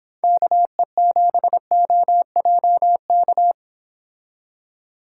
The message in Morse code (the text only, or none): KE7OJK